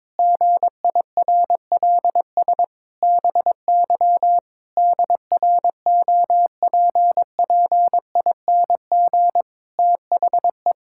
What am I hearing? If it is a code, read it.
Morse code, heard as a keyed tone, about 22 wpm, 700 Hz. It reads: GIRLS BY DROPPING THE
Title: Pangram Challenge v2 - 22wpm